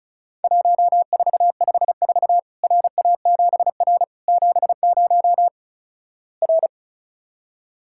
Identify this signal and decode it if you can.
Morse code, heard as a keyed tone, about 34 wpm, 700 Hz. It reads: 1454 RA7R 70 R
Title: Worked All Europe DX Contest - QTC Exchange - 35wpm